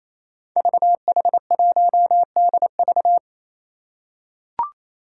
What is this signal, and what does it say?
Morse code, heard as a keyed tone, about 27 wpm, 700 Hz. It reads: VH1DV
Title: Five Challenging Letter-Number Combinations 28wpm